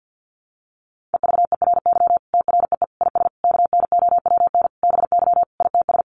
Morse code, heard as a keyed tone, about 50 wpm, 700 Hz. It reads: EVERY TREE IS KNOWN BY ITS
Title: Sentences from Top 1300 Words 50wpm